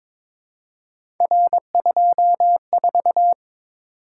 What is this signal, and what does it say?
Morse code, heard as a keyed tone, about 22 wpm, 700 Hz. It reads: R24
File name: practice / 22wpm / Three Letter-Number Combinations - 22wpm